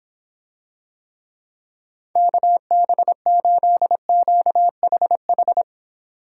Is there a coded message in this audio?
Yes: Morse code from 2.2 to 5.6 s, heard as a keyed tone, about 26 wpm, 700 Hz. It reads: KB8QHH